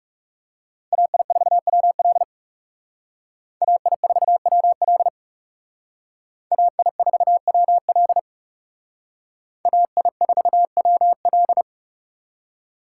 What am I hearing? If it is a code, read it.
Morse code, heard as a keyed tone, about 37 wpm, 700 Hz. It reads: AI4WL AI4WL AI4WL AI4WL
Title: Speed Racing - USA 2x2 Call Signs 30wpm